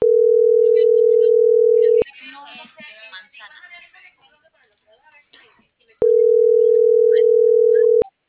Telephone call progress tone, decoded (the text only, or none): ringback tone